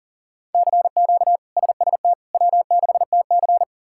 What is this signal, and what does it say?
Morse code, heard as a keyed tone, about 39 wpm, 700 Hz. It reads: CQ SST W6TC
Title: K1USN SST QSO Practice - 40wpm